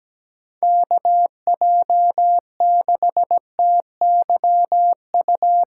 Morse code, heard as a keyed tone, about 20 wpm, 700 Hz. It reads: KJ6TYU
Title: Speed Racing - USA Call Signs 17wpm